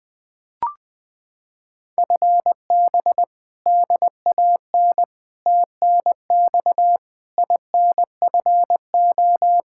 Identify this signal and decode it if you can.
Morse code, heard as a keyed tone, about 21 wpm, 700 Hz. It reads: FB DAN TNX INFO
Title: Practice QSO - Exchange 3 - More Info - 20wpm